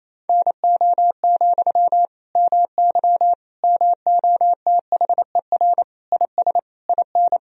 Morse code, heard as a keyed tone, about 28 wpm, 700 Hz. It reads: NO, MY MOTHER IS IN
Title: Sentences from Top 1000 Words - Review Part 2 - 28wpm